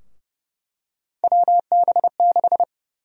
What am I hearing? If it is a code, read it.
Morse code, heard as a keyed tone, about 29 wpm, 700 Hz. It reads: WB6
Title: USA Call Sign Prefixes - 30wpm